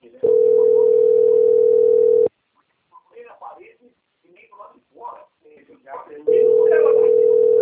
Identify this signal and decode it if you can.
Ringback tone (North America), 440 + 480 Hz, with speech.
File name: out- - - - - .11919